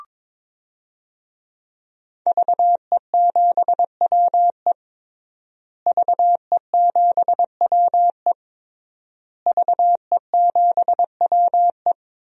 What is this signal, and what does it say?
Morse code, heard as a keyed tone, about 22 wpm, 700 Hz. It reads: VE7WE VE7WE VE7WE
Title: DXCC-100 Call Signs - 5 Characters - Repeated 3x - 22wpm